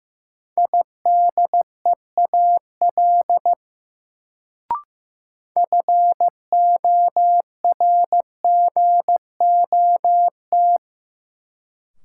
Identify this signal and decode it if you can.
Morse code, heard as a keyed tone, about 15 wpm, 700 Hz. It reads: IDEAL FORGOT
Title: Sentences from Top 1400 Words - New Word Drill Alt - 15wpm